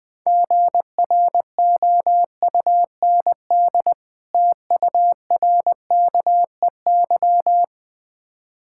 Morse code, heard as a keyed tone, about 20 wpm, 690 Hz. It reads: GROUND TURKEY
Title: Difficult Words with Hints - 20wpm